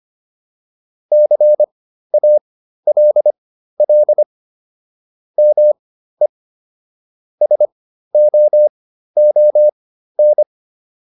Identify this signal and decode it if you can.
Morse code, heard as a keyed tone, about 25 wpm, spaced as for 13 wpm, 600 Hz. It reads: CALL ME SOON